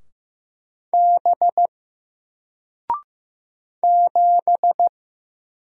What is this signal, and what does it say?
Morse code, heard as a keyed tone, about 15 wpm, 700 Hz. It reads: B 7
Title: Single Letter-Number 15wpm